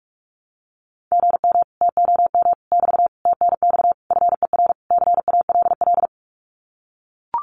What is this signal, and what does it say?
Morse code, heard as a keyed tone, about 44 wpm, 700 Hz. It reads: GM TOM <BT> TNX FER CALL
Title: Common QSO Phrases Encore 45wpm